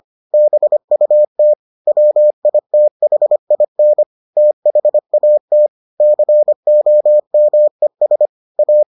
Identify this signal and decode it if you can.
Morse code, heard as a keyed tone, about 25 wpm, 600 Hz. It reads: BUT WITHIN THAT COMES A